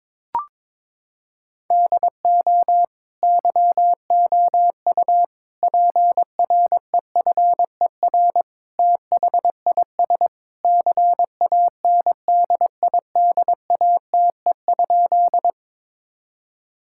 Morse code, heard as a keyed tone, about 22 wpm, 700 Hz. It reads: DO YOU PREFER THIS CANDIDATE?
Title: Sentences from Top 1600 Words 22wpm